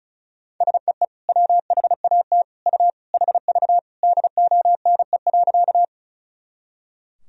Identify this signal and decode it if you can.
Morse code, heard as a keyed tone, about 34 wpm, 700 Hz. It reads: SEE WHAT U HV DONE.
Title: Sentences from Top 500 Words and Abbreviations 35wpm